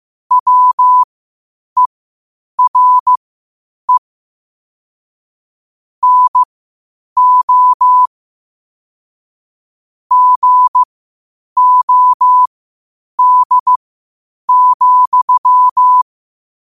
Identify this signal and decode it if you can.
Morse code, heard as a keyed tone, about 15 wpm, spaced as for 8 wpm, 1000 Hz. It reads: WERE NO GOD,
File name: qotdfiles / 8wpm